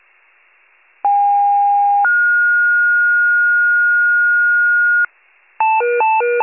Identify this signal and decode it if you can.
Two-tone page, heard as a one-second tone, then a three-second tone, about 810 then 1430 Hz, with noise. It is heard from 1.0 to 5.1 s.